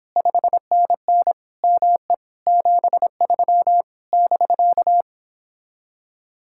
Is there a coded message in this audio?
Yes: Morse code from 0.2 to 5.0 s, heard as a keyed tone, about 29 wpm, 700 Hz. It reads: 5NN ME 73 <BK>